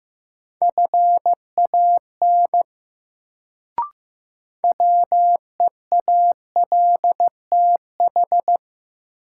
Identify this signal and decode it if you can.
Morse code, heard as a keyed tone, about 15 wpm, 700 Hz. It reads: FAN WEALTH
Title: Sentences from Top 1000 Words - Review Part 4 - New Word Drill - 15wpm